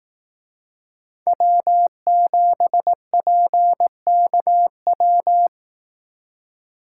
Morse code, heard as a keyed tone, about 18 wpm, 700 Hz. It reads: W7PKW
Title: Speed Racing - USA 1x3 Call Signs 15wpm